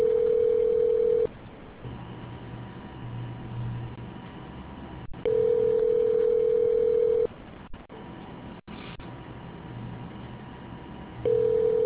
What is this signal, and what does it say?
Ringback tone (North America), 440 + 480 Hz.